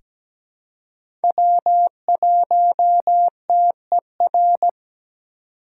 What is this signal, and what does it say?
Morse code, heard as a keyed tone, about 17 wpm, 700 Hz. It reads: W1TER
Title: K1USN SST QSO Practice - 17wpm